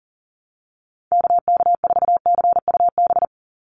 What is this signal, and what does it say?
Morse code, heard as a keyed tone, about 40 wpm, 700 Hz. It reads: KK4CUB
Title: USA Call Signs 40wpm